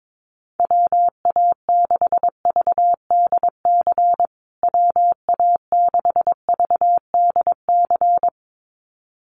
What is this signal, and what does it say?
Morse code, heard as a keyed tone, about 22 wpm, 700 Hz. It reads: WA6VDC WA6VDC
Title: USA Call Signs Repeated 2x 22wpm